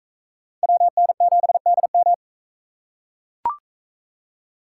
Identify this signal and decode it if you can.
Morse code, heard as a keyed tone, about 40 wpm, 700 Hz. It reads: WN7DM